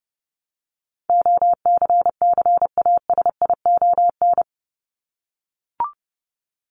Morse code, heard as a keyed tone, about 30 wpm, 700 Hz. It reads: OCCASION
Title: Top 1501-1750 Words 30wpm